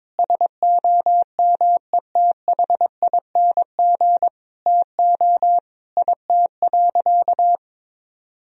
Morse code, heard as a keyed tone, about 22 wpm, 700 Hz. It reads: SOMETHING TO IT.
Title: Sentences from Top 1000 Words - Review Part 1 - 22wpm